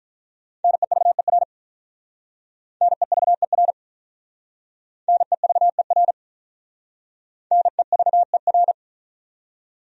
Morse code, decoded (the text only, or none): NEVER NEVER NEVER NEVER